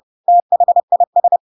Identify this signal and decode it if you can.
Morse code, heard as a keyed tone, about 30 wpm, 700 Hz. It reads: THIS
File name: Pinocchio / Pinocchio ch3 30 30 ik1hge